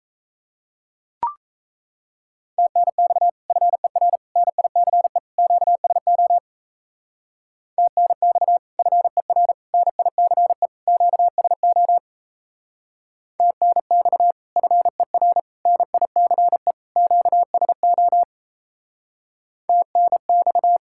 Morse code, heard as a keyed tone, about 34 wpm, 700 Hz. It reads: TNX FER NICE QSO TNX FER NICE QSO TNX FER NICE QSO TNX
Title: Speed-Racing Common QSO Phrases - 28wpm